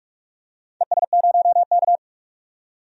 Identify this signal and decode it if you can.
Morse code, heard as a keyed tone, about 46 wpm, 700 Hz. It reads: ES0K
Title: DXCC-100 Call Signs - 4 Characters - 45wpm